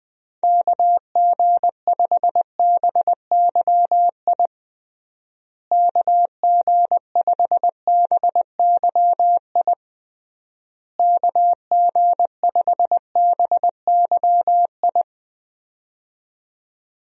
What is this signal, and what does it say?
Morse code, heard as a keyed tone, about 20 wpm, 700 Hz. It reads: KG5BYI KG5BYI KG5BYI
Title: USA Call Signs - 6 Characters - Repeated 3x - 20wpm